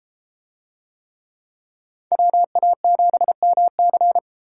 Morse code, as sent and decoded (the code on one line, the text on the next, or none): .-- .- --... -- -.-.
WA7MC